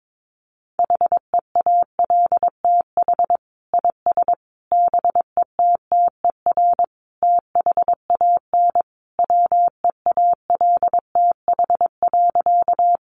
Morse code, heard as a keyed tone, about 22 wpm, 700 Hz. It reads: HEALTH IS BETTER THAN WEALTH.
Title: Sentences from Top 1200 Words 22wpm